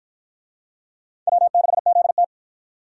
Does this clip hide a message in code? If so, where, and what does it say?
Yes: Morse code from 1.3 to 2.2 s, heard as a keyed tone, about 50 wpm, 700 Hz. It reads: W6ZT